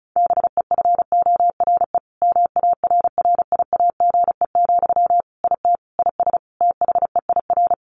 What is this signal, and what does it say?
Morse code, heard as a keyed tone, about 34 wpm, 700 Hz. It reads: BEFORE MARRIAGE, IT IS THEIR
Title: Sentences from Top 1500 Words 35wpm